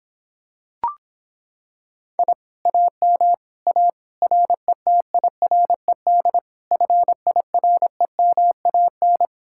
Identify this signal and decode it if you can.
Morse code, heard as a keyed tone, about 26 wpm, 700 Hz. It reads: I AM A RETIRED FIREMAN